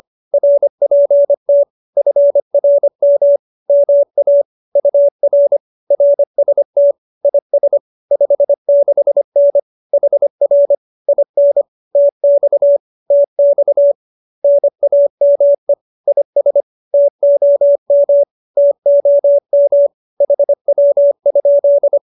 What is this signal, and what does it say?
Morse code, heard as a keyed tone, about 25 wpm, 570 Hz. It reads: RPT FRM MA UR RST IS 56N HR IN TX TX NAME IS TOM TOM HW?
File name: qso201b 25